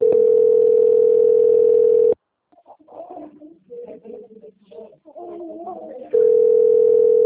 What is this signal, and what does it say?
Ringback tone (North America), 440 + 480 Hz.